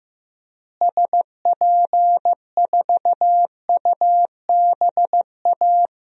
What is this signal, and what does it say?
Morse code, heard as a keyed tone, about 15 wpm, 700 Hz. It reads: SP4UBA
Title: DXCC-100 Call Signs - 6 Characters - Repeated 2x - 15wpm